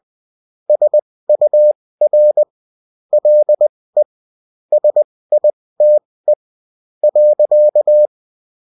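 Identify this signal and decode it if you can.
Morse code, heard as a keyed tone, about 20 wpm, spaced as for 15 wpm, 600 Hz. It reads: SUR LE SITE.
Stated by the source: Bulletin in text and CW form.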